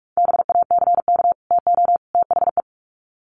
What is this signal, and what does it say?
Morse code, heard as a keyed tone, about 44 wpm, 700 Hz. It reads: BACK TO THE
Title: Sets of 3 Words 45wpm